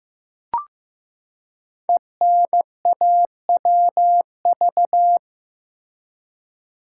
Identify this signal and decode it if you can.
Morse code, heard as a keyed tone, about 15 wpm, 700 Hz. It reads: ENAWV
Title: Code Groups - 5 Letters - 15wpm